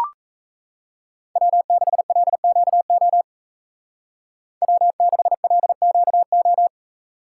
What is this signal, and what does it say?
Morse code, heard as a keyed tone, about 34 wpm, 700 Hz. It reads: W6LQO W6LQO
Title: Speed Racing - USA Call Signs 28wpm